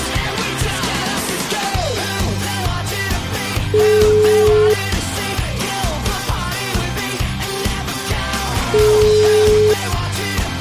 Ringback tone (Europe, 425 Hz), with noise.